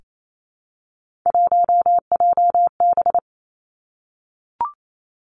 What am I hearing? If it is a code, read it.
Morse code, heard as a keyed tone, about 28 wpm, 700 Hz. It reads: 1JB
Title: Three Challenging Letter-Number Combinations 28wpm